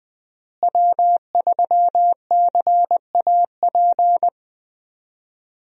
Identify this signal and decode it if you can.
Morse code, heard as a keyed tone, about 21 wpm, 700 Hz. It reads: W3CAP
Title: USA Call Signs - 5 Characters - Repeated 3x - 20wpm